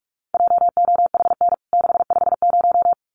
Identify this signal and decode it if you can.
Morse code, heard as a keyed tone, about 44 wpm, 700 Hz. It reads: JOHN 650
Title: CWT Exchange 45wpm